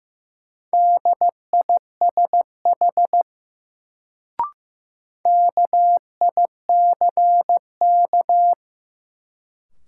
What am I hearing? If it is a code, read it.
Morse code, heard as a keyed tone, about 15 wpm, 700 Hz. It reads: DISH KICK
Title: Sentences from Top 1300 Words - New Word Drill - 15wpm